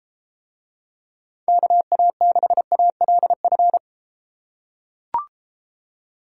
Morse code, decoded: KA6ALF

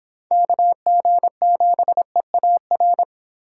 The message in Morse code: KG7EAR

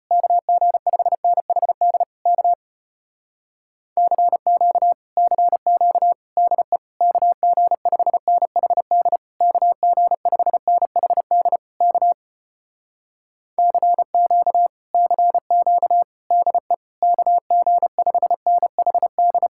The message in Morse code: KG5NHD K CQ CQ DE KG5NHD KG5NHD K CQ CQ DE KG5NHD

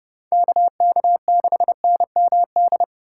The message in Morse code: KK6NMD

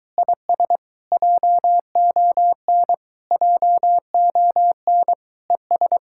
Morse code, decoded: IS JON JON ES

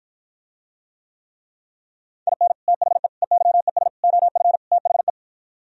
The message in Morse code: IN THE EYES OF THE

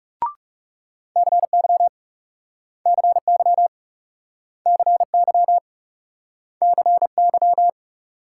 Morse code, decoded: CY CY CY CY

